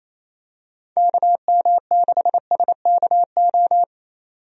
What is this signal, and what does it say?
Morse code, heard as a keyed tone, about 28 wpm, 700 Hz. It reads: KM6SKO
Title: USA Call Signs - 6 Characters - 28wpm